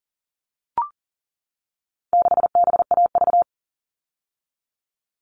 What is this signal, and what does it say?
Morse code, heard as a keyed tone, about 40 wpm, 700 Hz. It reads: 6BAV
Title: All Cut Numbers - 4-Digit 40wpm